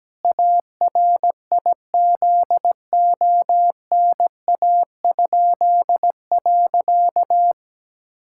Morse code, heard as a keyed tone, about 17 wpm, 700 Hz. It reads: ARIZONA?.
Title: Pangram Challenge v2 - 17wpm